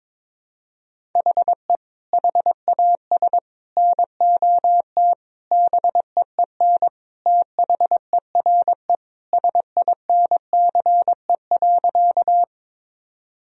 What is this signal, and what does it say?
Morse code, heard as a keyed tone, about 22 wpm, 700 Hz. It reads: HE HAS NOT BEEN THERE SINCE.
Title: Sentences from Top 1000 Words - Review Part 1 - 22wpm